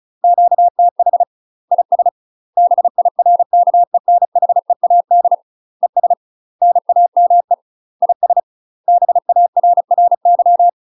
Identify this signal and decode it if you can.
Morse code, heard as a keyed tone, about 34 wpm, 700 Hz. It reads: QTH IS BIRKENHEAD ES NAME IS BARRY
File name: UK MASTER QSOs TRAINING / 35WPM / UK MASTER QSOs 0039